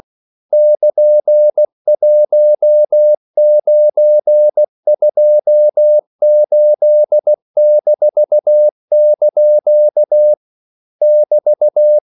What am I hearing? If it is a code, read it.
Morse code, heard as a keyed tone, about 16 wpm, 600 Hz. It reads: <KN>1928-) <BT>
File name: CW / Practice / EnglishQuotations0216